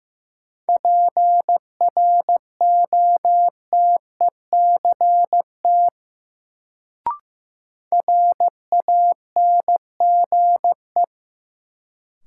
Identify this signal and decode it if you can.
Morse code, heard as a keyed tone, about 15 wpm, 700 Hz. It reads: PROTECT RANGE
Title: Sentences from Top 900 Words - New Word Drill - 15wpm